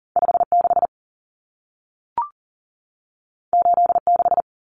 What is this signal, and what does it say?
Morse code, heard as a keyed tone, about 40 wpm, 700 Hz. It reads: 56 86
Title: Common Cut Numbers - 2-Digit 40wpm